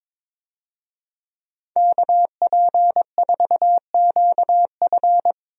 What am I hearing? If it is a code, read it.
Morse code, heard as a keyed tone, about 22 wpm, 700 Hz. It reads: KP4QF
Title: DXCC-100 Call Signs - 5 Characters - Repeated 2x - 22wpm